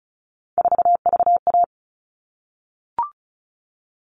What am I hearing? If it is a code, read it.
Morse code, heard as a keyed tone, about 34 wpm, 700 Hz. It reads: 4VA
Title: All Cut Numbers - 3-Digit 35wpm